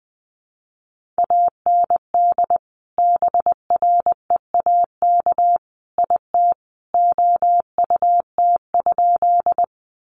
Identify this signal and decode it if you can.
Morse code, heard as a keyed tone, about 20 wpm, 700 Hz. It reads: AND BREAK IT OUT?
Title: Sentences from Top 900 Words 20wpm